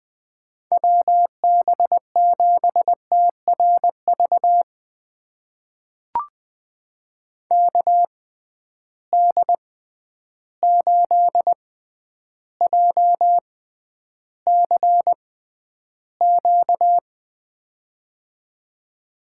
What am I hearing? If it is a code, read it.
Morse code, heard as a keyed tone, about 20 wpm, spaced as for 10 wpm, 700 Hz. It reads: WB7TRV KD8JCQ
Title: USA Call Signs - 6 Characters - 6x Character Spacing - 20wpm